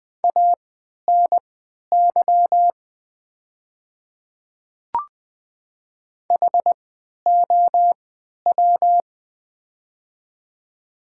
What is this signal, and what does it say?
Morse code, heard as a keyed tone, about 20 wpm, spaced as for 11 wpm, 700 Hz. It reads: ANY HOW
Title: 3x Character Spacing - Top 200 Words - 20wpm